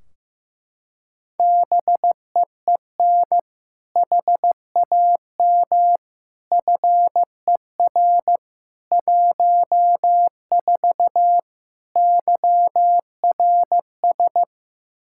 Morse code, heard as a keyed tone, about 15 wpm, 700 Hz. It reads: BEEN HAM FER 14 YRS
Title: Practice QSO - Exchange 3 - More Info - 15wpm